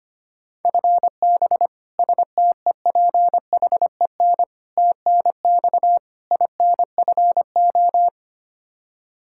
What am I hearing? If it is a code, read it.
Morse code, heard as a keyed tone, about 25 wpm, 700 Hz. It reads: FB STEPHEN TNX INFO
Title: Practice QSO - Exchange 3 - More Info - 25wpm